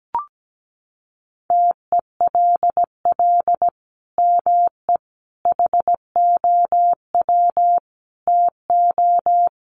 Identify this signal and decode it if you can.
Morse code, heard as a keyed tone, about 17 wpm, 700 Hz. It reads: TELL ME HOW TO
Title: Sentences from Top 1500 Words 17wpm